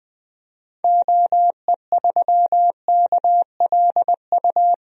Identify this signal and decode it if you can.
Morse code, heard as a keyed tone, about 20 wpm, 700 Hz. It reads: OE3KLU
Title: DXCC-100 Call Signs - 6 Characters - Repeated 3x - 20wpm